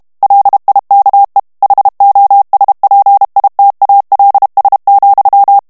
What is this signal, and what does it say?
Morse code, heard as a keyed tone, about 32 wpm, 800 Hz. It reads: LIKE HOSPITALS,
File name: ANGLAIS texte clair 2 vitesse 30 mots minute